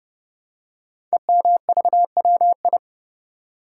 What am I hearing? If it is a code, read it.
Morse code, heard as a keyed tone, about 30 wpm, 700 Hz. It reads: EMVWI
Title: Code Groups - 5 Letters - 30wpm